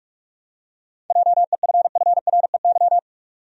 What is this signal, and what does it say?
Morse code, heard as a keyed tone, about 46 wpm, 700 Hz. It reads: JEFFREY
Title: USA Common Names - 45wpm